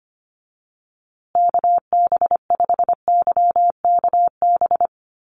Morse code, heard as a keyed tone, about 25 wpm, 700 Hz. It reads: KB5YKB
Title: USA Call Signs Repeated 2x 25wpm